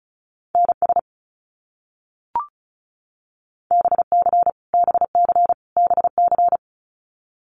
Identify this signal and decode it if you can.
Morse code, heard as a keyed tone, about 36 wpm, 700 Hz. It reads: NS BC BC BC
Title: Canadian Provinces and Territories - Repeated 3x - 35wpm